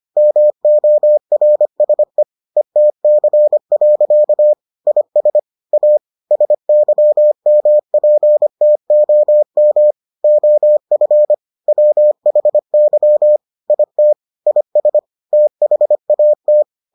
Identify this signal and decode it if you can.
Morse code, heard as a keyed tone, about 25 wpm, 600 Hz. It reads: MORSE ETC. IS A SYMPTOM OF WHY IT IS THAT